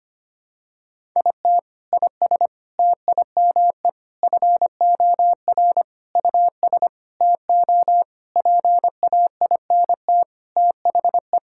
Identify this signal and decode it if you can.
Morse code, heard as a keyed tone, about 25 wpm, 700 Hz. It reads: IT IS TIME FOR US TO PAINT THE
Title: Sentences from Top 1000 Words - Review Part 6 - 25wpm